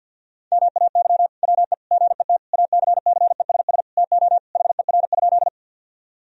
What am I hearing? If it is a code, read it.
Morse code, heard as a keyed tone, about 52 wpm, 700 Hz. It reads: MAY WE GET ACCESS TO HER?
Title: Sentences from Top 1000 Words - Review Part 4 - 50wpm